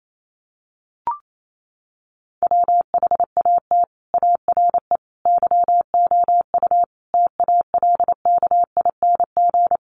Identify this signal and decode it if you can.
Morse code, heard as a keyed tone, about 28 wpm, 700 Hz. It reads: WHAT ARE YOU TALKING
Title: Sets of 5 Words 28wpm